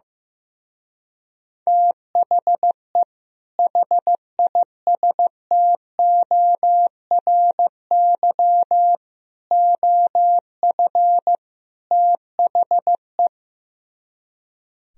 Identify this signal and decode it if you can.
Morse code, heard as a keyed tone, about 15 wpm, 700 Hz. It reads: THE HISTORY OF THE
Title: Speed-Racing - Sets of 4 Words 15wpm